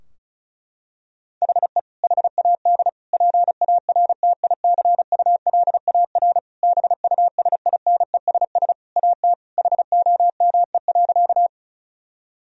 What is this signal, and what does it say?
Morse code, heard as a keyed tone, about 36 wpm, 700 Hz. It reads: HE HAD PARTICULAR BUSINESS AT HOME.
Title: Sentences from Top 1000 Words - Review Part 4 - 35wpm